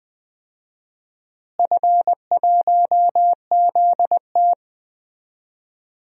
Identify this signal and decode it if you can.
Morse code, heard as a keyed tone, about 20 wpm, 700 Hz. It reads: F1ZT
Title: DXCC-100 Call Signs - 4 Characters - Repeated 2x - 20wpm